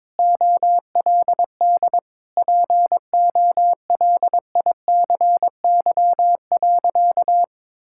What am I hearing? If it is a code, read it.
Morse code, heard as a keyed tone, about 22 wpm, 700 Hz. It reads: OLD POLICY.
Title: Sentences from Top 1000 Words - Review Part 5 - 22wpm